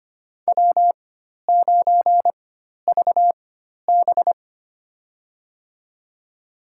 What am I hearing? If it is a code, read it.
Morse code, heard as a keyed tone, about 25 wpm, spaced as for 12 wpm, 700 Hz. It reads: W9VB
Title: USA Call Signs - 4x Character Spacing - 25wpm